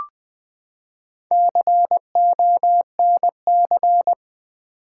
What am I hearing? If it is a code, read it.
Morse code, heard as a keyed tone, about 20 wpm, 700 Hz. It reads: CONC
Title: Build A Word - 4 Letters At A Time - 10-Letter Words - 20wpm